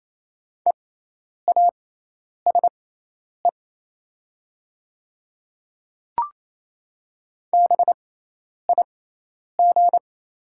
Morse code, heard as a keyed tone, about 28 wpm, spaced as for 10 wpm, 700 Hz. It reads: EASE BIG
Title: 6x Character Spacing - Top 300 Words - 28wpm